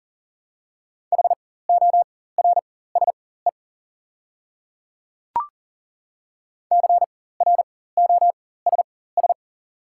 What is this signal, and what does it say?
Morse code, heard as a keyed tone, about 40 wpm, spaced as for 19 wpm, 700 Hz. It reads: HORSE CROSS
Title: 4x Character Spacing - Top 300 Words - 40wpm